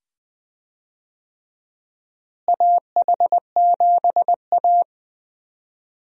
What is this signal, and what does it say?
Morse code, heard as a keyed tone, about 20 wpm, 700 Hz. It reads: AH7A
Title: USA Call Signs - 4 Characters - Repeated 2x - 20wpm